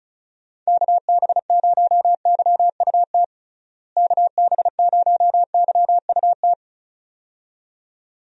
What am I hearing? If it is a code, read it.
Morse code, heard as a keyed tone, about 36 wpm, 700 Hz. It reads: KB0YUT KB0YUT
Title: USA Call Signs - 6 Characters - Repeated 2x - 35wpm